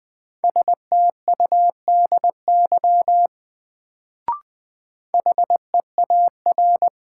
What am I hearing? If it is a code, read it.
Morse code, heard as a keyed tone, about 20 wpm, 700 Hz. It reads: STUDY HEAR
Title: Sentences from Top 300 Words - New Word Drill - 20wpm